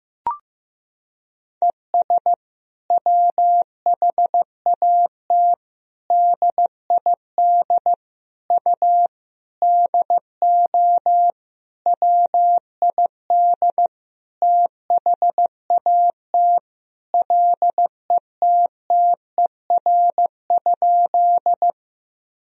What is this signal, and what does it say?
Morse code, heard as a keyed tone, about 15 wpm, 700 Hz. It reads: ES WHAT DID U DO WID THAT LETTER?
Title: Sentences from Top 300 Words and Abbreviations 15wpm